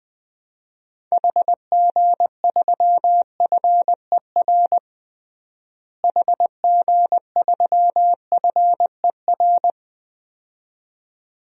Morse code, heard as a keyed tone, about 20 wpm, 700 Hz. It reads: HG3FER HG3FER
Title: DXCC-100 Call Signs - 6 Characters - Repeated 2x - 20wpm